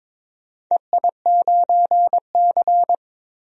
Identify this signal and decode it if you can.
Morse code, heard as a keyed tone, about 22 wpm, 700 Hz. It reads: EI9C
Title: DXCC-100 Call Signs - 4 Characters - 22wpm